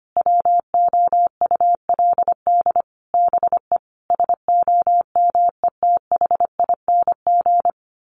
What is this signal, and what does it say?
Morse code, heard as a keyed tone, about 25 wpm, 700 Hz. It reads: WOULD BE SOMETHING
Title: Sentences from Top 1200 Words 25wpm